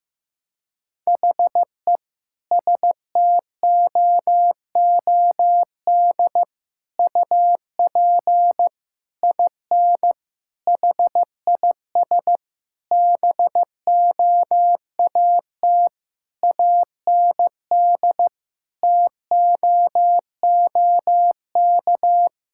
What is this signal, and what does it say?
Morse code, heard as a keyed tone, about 15 wpm, 700 Hz. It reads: HE STOOD UP IN HIS BOAT AND TOOK
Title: Sentences from Top 1600 Words 15wpm